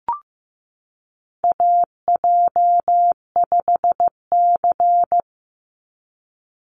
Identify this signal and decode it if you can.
Morse code, heard as a keyed tone, about 15 wpm, 700 Hz. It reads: AJ5C
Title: NA SOTA Chaser Callsigns 15wpm